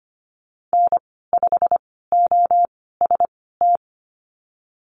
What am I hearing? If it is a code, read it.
Morse code, heard as a keyed tone, about 25 wpm, spaced as for 16 wpm, 700 Hz. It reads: N5OST
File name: Lesson 15g - Number 5 - Call Signs 25wpm